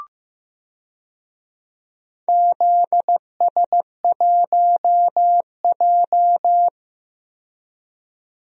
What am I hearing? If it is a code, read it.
Morse code, heard as a keyed tone, about 15 wpm, 700 Hz. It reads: ZS1J
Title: DXCC-100 Call Signs - 4 Characters - 15wpm